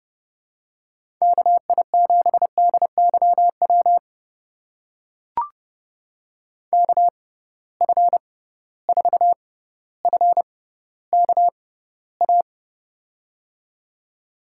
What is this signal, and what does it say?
Morse code, heard as a keyed tone, about 30 wpm, spaced as for 15 wpm, 700 Hz. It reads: KI7DYW KF4FKA
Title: USA Call Signs - 6 Characters - 6x Character Spacing - 30wpm